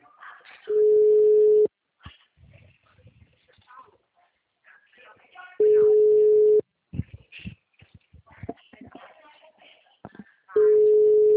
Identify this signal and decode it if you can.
Ringback tone (Europe, 425 Hz), with speech.